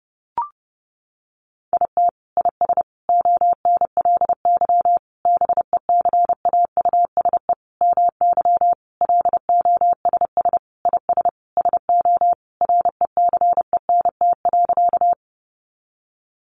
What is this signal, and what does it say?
Morse code, heard as a keyed tone, about 30 wpm, 700 Hz. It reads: IT IS ONLY BECAUSE MY LOSS IS SO RECENT.
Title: Sentences from Top 1400 Words 30wpm